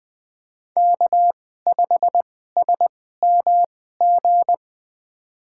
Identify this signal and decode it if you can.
Morse code, heard as a keyed tone, about 20 wpm, spaced as for 14 wpm, 700 Hz. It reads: K5SMG
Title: USA Call Signs - 5 Characters - 2x Character Spacing - 20wpm